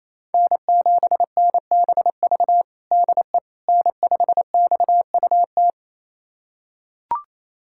Morse code, heard as a keyed tone, about 28 wpm, 700 Hz. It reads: N7NBV DE N5XUT
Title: USA Callsign DE Callsign - 5 Characters - 28wpm